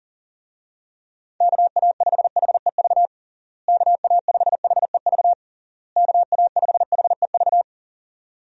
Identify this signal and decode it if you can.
Morse code, heard as a keyed tone, about 37 wpm, 700 Hz. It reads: KA5HEV KA5HEV KA5HEV
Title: USA Call Signs - 6 Characters - Repeated 3x - 40wpm